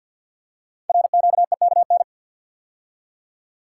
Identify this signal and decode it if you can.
Morse code, heard as a keyed tone, about 48 wpm, 700 Hz. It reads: RQEKN
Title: Code Groups - 5 Letters - 50wpm